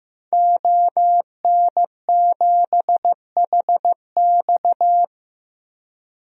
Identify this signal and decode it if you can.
Morse code, heard as a keyed tone, about 15 wpm, 700 Hz. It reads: ON7HX
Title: DXCC-100 Call Signs - 5 Characters - 15wpm